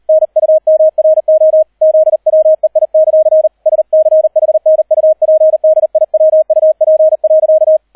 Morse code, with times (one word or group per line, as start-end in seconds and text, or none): 0.1-1.6 NUMRO
1.8-3.5 ZWEI;
3.7-7.8 SCHNUPDIWUP.